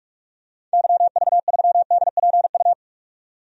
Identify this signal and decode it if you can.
Morse code, heard as a keyed tone, about 44 wpm, 700 Hz. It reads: YV3DPU
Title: DXCC-100 Call Signs - 6 Characters - 45wpm